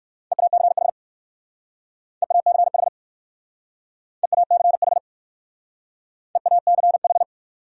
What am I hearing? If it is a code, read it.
Morse code, heard as a keyed tone, about 57 wpm, 700 Hz. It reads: EACH EACH EACH EACH